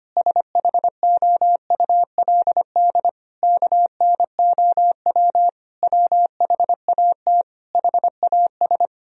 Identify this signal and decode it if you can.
Morse code, heard as a keyed tone, about 25 wpm, 700 Hz. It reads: SHOULD KNOW WHAT HAS
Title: Sentences from Top 1000 Words - Review Part 1 - 25wpm